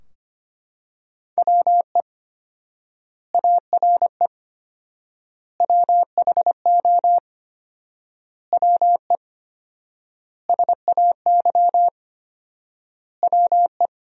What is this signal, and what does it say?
Morse code, heard as a keyed tone, about 25 wpm, 700 Hz. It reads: WE ARE WHO WE SAY WE
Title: 4x Word-Spacing - Sentences from Top 200 Words - 25wpm